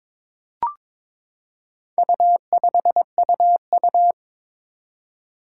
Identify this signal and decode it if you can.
Morse code, heard as a keyed tone, about 22 wpm, 700 Hz. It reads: U5UU
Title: Four Challenging Letter-Number Combinations 22wpm